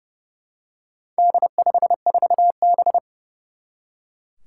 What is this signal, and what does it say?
Morse code, heard as a keyed tone, about 30 wpm, 700 Hz. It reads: D54B
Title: Four Challenging Letter-Number Combinations 30wpm